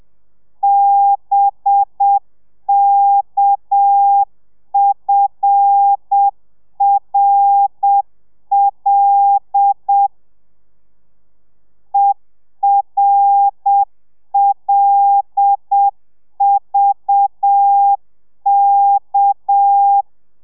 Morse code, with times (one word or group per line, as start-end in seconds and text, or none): 0.6-10.1 BKFRL
11.9-20.0 ERLVK